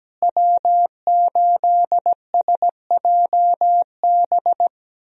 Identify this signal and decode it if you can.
Morse code, heard as a keyed tone, about 17 wpm, 700 Hz. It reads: W8SJB
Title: USA Call Signs - 5 Characters - Repeated 3x - 17wpm